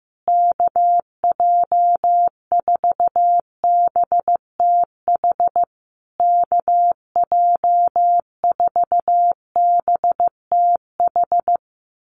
Morse code, heard as a keyed tone, about 15 wpm, 700 Hz. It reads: KJ4BTH KJ4BTH
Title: USA Call Signs Repeated 3x 15wpm